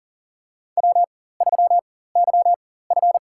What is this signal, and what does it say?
Morse code, heard as a keyed tone, about 40 wpm, spaced as for 19 wpm, 700 Hz. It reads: W3YF
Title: USA Call Signs - 4 Characters - 4x Character Spacing - 40wpm